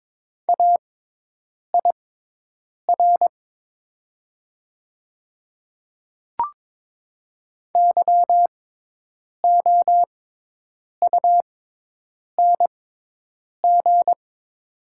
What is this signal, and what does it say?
Morse code, heard as a keyed tone, about 22 wpm, spaced as for 8 wpm, 700 Hz. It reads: AIR YOUNG
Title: 6x Character Spacing - Top 500 Words - 22wpm